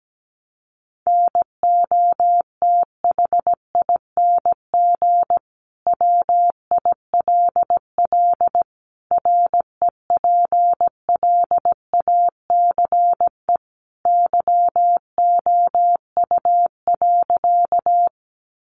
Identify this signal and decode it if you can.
Morse code, heard as a keyed tone, about 17 wpm, 700 Hz. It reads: NOTHING WILL REPLACE YOU.
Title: Sentences from Top 1400 Words 17wpm